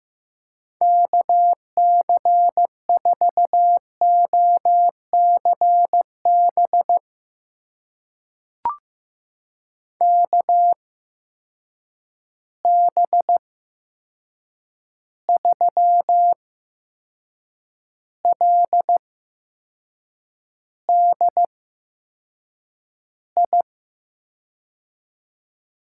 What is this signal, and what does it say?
Morse code, heard as a keyed tone, about 15 wpm, spaced as for 6 wpm, 700 Hz. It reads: KC4OCB KB3LDI
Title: USA Call Signs - 6 Characters - 8x Character Spacing - 15wpm